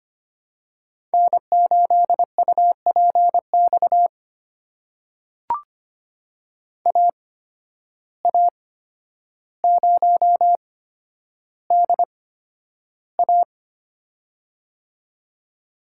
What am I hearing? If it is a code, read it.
Morse code, heard as a keyed tone, about 25 wpm, spaced as for 11 wpm, 700 Hz. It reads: N8UPX AA0DA
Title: USA Call Signs - 5 Characters - 8x Character Spacing - 25wpm